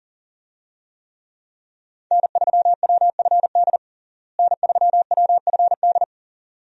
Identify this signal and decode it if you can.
Morse code, heard as a keyed tone, about 40 wpm, 700 Hz. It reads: N3WFD N3WFD
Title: USA Call Signs - 5 Characters - Repeated 2x - 40wpm